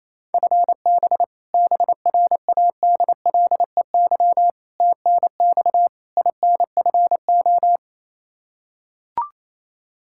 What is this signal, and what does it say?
Morse code, heard as a keyed tone, about 28 wpm, 700 Hz. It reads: FB BRADLEY TNX INFO
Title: Practice QSO - Exchange 3 - More Info - 28wpm